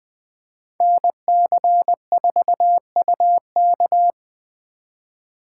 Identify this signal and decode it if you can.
Morse code, heard as a keyed tone, about 21 wpm, 700 Hz. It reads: NC4UK